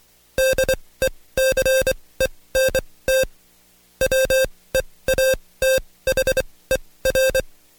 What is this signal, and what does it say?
Morse code, heard as a keyed tone, about 23 wpm, spaced as for 17 wpm, 530 Hz. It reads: DECENT WEATHER
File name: Code Quick / AudioDownld2 / Seg135-Stepper8 17WPM